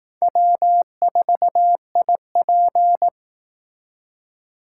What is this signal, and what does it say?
Morse code, heard as a keyed tone, about 18 wpm, 700 Hz. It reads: W4IP
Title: Speed Racing - USA 1x2 Call Signs 15wpm